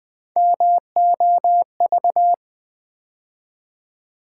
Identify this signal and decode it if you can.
Morse code, heard as a keyed tone, about 20 wpm, 700 Hz. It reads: MOV
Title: Build A Word - 3 Letters At A Time - 6-Letter Words - 20wpm